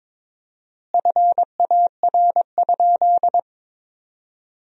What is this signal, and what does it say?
Morse code, heard as a keyed tone, about 23 wpm, 700 Hz. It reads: FAR?
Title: 4x Word-Spacing - Sentences from Top 300 Words - 22wpm